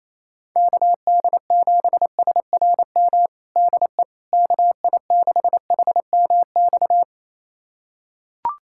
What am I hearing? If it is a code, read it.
Morse code, heard as a keyed tone, about 28 wpm, 700 Hz. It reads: KD7SRM DE KI6HMX
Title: USA Callsign DE Callsign 28wpm